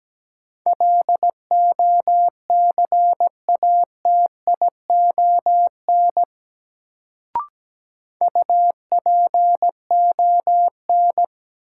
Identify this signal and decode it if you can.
Morse code, heard as a keyed tone, about 17 wpm, 700 Hz. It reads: LOCATION UPON
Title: Sentences from Top 1000 Words - Review Part 6 - New Word Drill - 17wpm